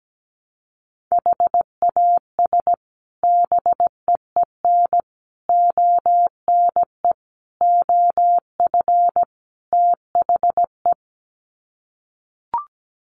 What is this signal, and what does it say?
Morse code, heard as a keyed tone, about 17 wpm, 700 Hz. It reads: HAS BEEN ONE OF THE
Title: Sets of 5 Words Encore 17wpm